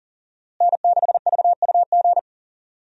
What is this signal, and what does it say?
Morse code, heard as a keyed tone, about 40 wpm, 700 Hz. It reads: N6VUG
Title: USA Call Signs - 5 Characters - Repeated 2x - 40wpm